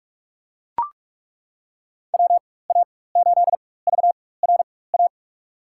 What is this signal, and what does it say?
Morse code, heard as a keyed tone, about 42 wpm, spaced as for 20 wpm, 700 Hz. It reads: WA8VRA
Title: USA Call Signs - 6 Characters - 4x Character Spacing - 45wpm